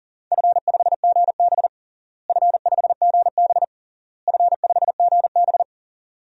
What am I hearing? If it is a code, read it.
Morse code, heard as a keyed tone, about 37 wpm, 700 Hz. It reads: F5GB F5GB F5GB
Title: DXCC-100 Call Signs - 4 Characters - Repeated 3x - 40wpm